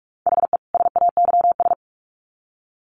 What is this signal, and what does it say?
Morse code, heard as a keyed tone, about 44 wpm, 700 Hz. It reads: HE SAYS
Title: Sets of 2 Words 45wpm